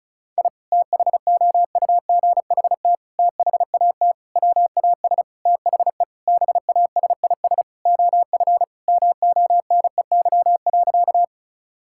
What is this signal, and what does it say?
Morse code, heard as a keyed tone, about 36 wpm, 700 Hz. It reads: I THOUGHT THAT WAS THE BASIS OF MONEY.
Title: Sentences from Top 1000 Words - Review Part 5 - 35wpm